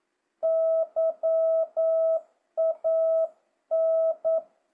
Morse code, heard as a keyed tone, about 9 wpm, 650 Hz. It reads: YAN
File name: jh8cbh / 2025cw